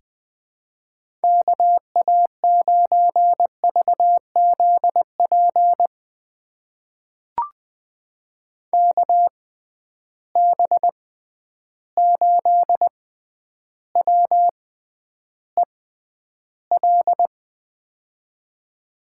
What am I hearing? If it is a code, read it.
Morse code, heard as a keyed tone, about 20 wpm, spaced as for 10 wpm, 700 Hz. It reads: KA9VZP KB8WEL
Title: USA Call Signs - 6 Characters - 6x Character Spacing - 20wpm